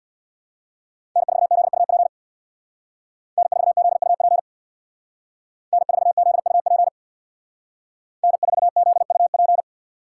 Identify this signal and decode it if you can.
Morse code, heard as a keyed tone, about 59 wpm, 700 Hz. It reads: N4ZUP N4ZUP N4ZUP N4ZUP
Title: Speed Racing - USA 1x3 Call Signs 50wpm